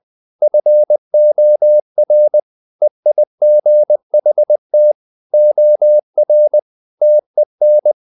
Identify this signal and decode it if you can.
Morse code, heard as a keyed tone, about 20 wpm, 600 Hz. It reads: FOR EIGHT OR TEN
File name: the old man and the sea part 2 20wpm